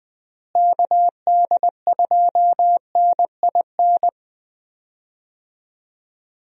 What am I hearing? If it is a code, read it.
Morse code, heard as a keyed tone, about 20 wpm, 700 Hz. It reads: KD2NIN
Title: USA Call Signs - 6 Characters - Repeated 3x - 20wpm